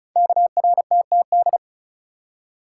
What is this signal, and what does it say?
Morse code, heard as a keyed tone, about 36 wpm, 700 Hz. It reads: KRTTD
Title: Code Groups - 5 Letter:Numbers - 35wpm